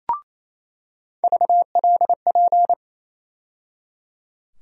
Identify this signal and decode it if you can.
Morse code, heard as a keyed tone, about 28 wpm, 700 Hz. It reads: VLP
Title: USA Call Sign Postfixes - 3 Letters 28wpm